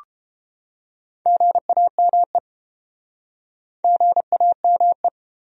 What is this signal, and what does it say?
Morse code, heard as a keyed tone, about 31 wpm, 700 Hz. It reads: GAME GAME